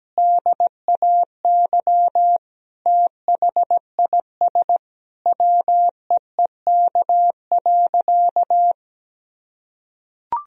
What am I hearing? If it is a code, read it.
Morse code, heard as a keyed tone, about 17 wpm, 700 Hz. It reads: DAY THIS WEEK.
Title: Sentences from Top 1000 Words - Review Part 2 - 17wpm